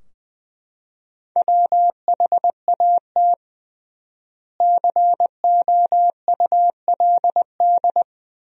Morse code, heard as a keyed tone, about 20 wpm, 700 Hz. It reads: WHAT COULD
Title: 3x Word-Spacing - Sentences from Top 100 Words - 20wpm